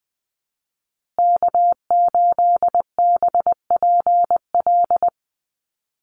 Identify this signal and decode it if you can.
Morse code, heard as a keyed tone, about 20 wpm, 700 Hz. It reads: K8BPL
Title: USA Call Signs Repeated 2x 20wpm